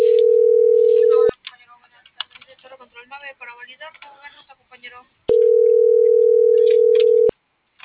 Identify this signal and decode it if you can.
Ringback tone (North America), 440 + 480 Hz.
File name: CENTRAL MABE / out xquenda.trinidad